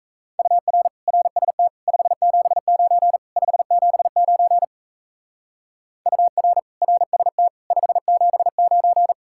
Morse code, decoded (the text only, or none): UR RST 579 579 UR RST 579